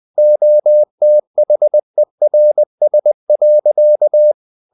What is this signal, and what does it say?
Morse code, heard as a keyed tone, about 20 wpm, 600 Hz. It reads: OTHERS.
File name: the old man and the sea part 2 20wpm